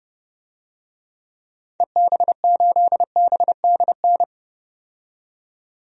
Morse code, heard as a keyed tone, about 30 wpm, 700 Hz. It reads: EB8BDN
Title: DXCC-100 Call Signs - 6 Characters - 30wpm